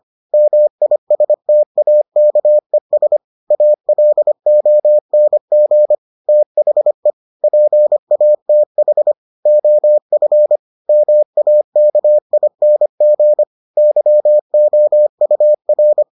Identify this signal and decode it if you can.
Morse code, heard as a keyed tone, about 25 wpm, 600 Hz. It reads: MISTAKES ALONG THE PATH OF MAKING YOUR